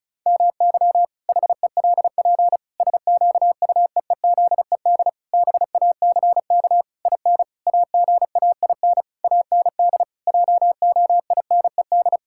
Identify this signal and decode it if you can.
Morse code, heard as a keyed tone, about 36 wpm, 700 Hz. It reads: MY HELP SQUEEZED BACK IN AGAIN AND JOINED
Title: Pangram Challenge v2 - 35wpm